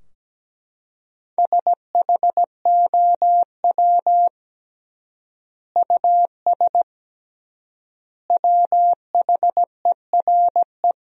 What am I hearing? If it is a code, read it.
Morse code, heard as a keyed tone, about 17 wpm, 700 Hz. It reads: SHOW US WHERE
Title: 3x Word-Spacing - Sentences from Top 200 Words - 17wpm